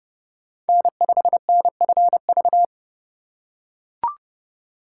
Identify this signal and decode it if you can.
Morse code, heard as a keyed tone, about 32 wpm, 700 Hz. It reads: N5NFV
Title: USA Call Signs - 5 Characters - 4x Character Spacing - 30wpm